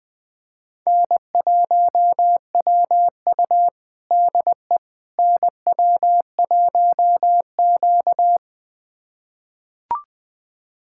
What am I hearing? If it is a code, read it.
Morse code, heard as a keyed tone, about 20 wpm, 700 Hz. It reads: N1WU DE NW1Q
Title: USA Callsign DE Callsign - 4 Characters - 20wpm